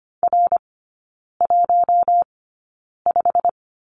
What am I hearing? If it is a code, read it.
Morse code, heard as a keyed tone, about 25 wpm, spaced as for 9 wpm, 690 Hz. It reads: R15
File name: Lesson 15e - Number 5 - Three Characters 25wpm